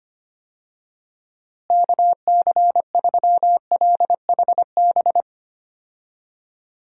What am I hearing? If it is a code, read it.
Morse code, heard as a keyed tone, about 25 wpm, 700 Hz. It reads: KC3LHB
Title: USA Call Signs - 6 Characters - Repeated 3x - 25wpm